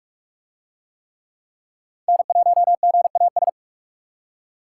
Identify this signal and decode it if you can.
Morse code, heard as a keyed tone, about 44 wpm, 700 Hz. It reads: N1GAS